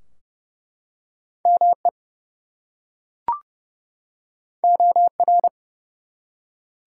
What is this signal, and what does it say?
Morse code, heard as a keyed tone, about 30 wpm, 700 Hz. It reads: ME OR
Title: Two Letter Words - 30wpm